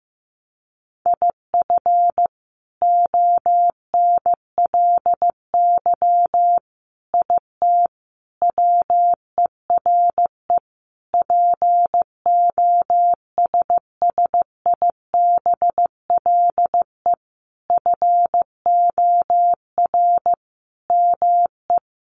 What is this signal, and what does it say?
Morse code, heard as a keyed tone, about 15 wpm, 700 Hz. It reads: IF ONLY IT WERE POSSIBLE FOR ME
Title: Sentences from Top 500 Words 15wpm